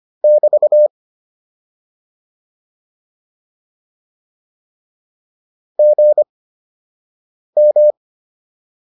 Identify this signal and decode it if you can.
Morse code, heard as a keyed tone, about 25 wpm, spaced as for 6 wpm, 600 Hz. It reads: <BT> GM
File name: QSO 204 06